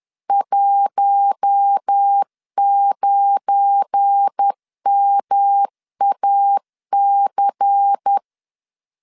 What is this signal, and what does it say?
Morse code, heard as a keyed tone, about 10 wpm, 800 Hz. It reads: 19MAC
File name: LEZIONI / 9 2 10wpm 2736